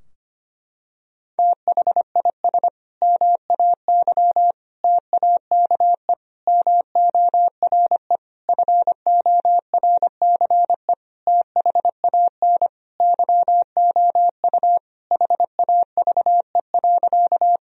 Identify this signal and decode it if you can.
Morse code, heard as a keyed tone, about 25 wpm, 700 Hz. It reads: THIS MAY TAKE MORE FORCE THAN YOU HAVE.
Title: Sentences from Top 1000 Words - Review Part 3 - 25wpm